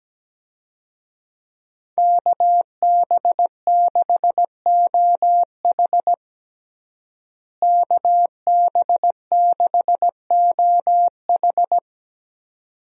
Morse code, heard as a keyed tone, about 17 wpm, 700 Hz. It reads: KB6OH KB6OH
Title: USA Call Signs - 5 Characters - Repeated 2x - 17wpm